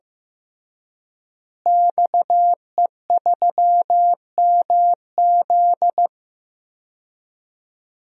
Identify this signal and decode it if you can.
Morse code, heard as a keyed tone, about 15 wpm, 700 Hz. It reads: XE3MZ
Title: DXCC-100 Call Signs - 5 Characters - Repeated 2x - 15wpm